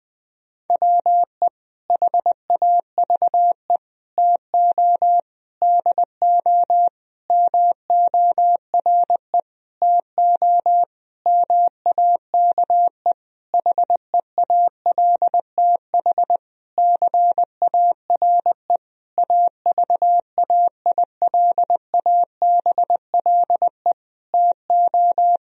Morse code, heard as a keyed tone, about 20 wpm, 700 Hz. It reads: WE HAVE TO DO MORE TO MAKE HEALTH CARE AVAILABLE TO
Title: Sentences from Top 1000 Words - Review Part 3 - 20wpm